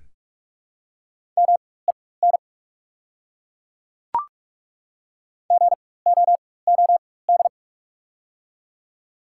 Morse code, heard as a keyed tone, about 47 wpm, spaced as for 21 wpm, 690 Hz. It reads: MEN GOOD
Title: 4x Character Spacing - Top 200 Words - 45wpm